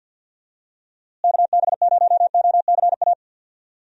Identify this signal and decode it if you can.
Morse code, heard as a keyed tone, about 49 wpm, 700 Hz. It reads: KB0OCA
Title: USA Call Signs - 6 Characters - Repeated 2x - 50wpm